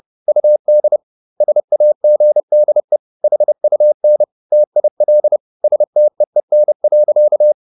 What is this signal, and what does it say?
Morse code, heard as a keyed tone, about 30 wpm, 600 Hz. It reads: UD SAGDE HUN TIL STEEN.
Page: Ferie 30 wpm